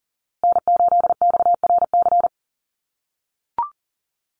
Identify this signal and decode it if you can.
Morse code, heard as a keyed tone, about 40 wpm, 700 Hz. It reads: N8XRC
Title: USA Call Signs Repeated 3x 40wpm